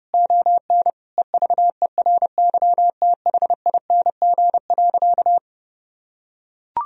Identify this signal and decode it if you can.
Morse code, heard as a keyed tone, about 30 wpm, 700 Hz. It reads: ON EVERYTHING.
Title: Sentences from Top 1000 Words - Review Part 1 - 30wpm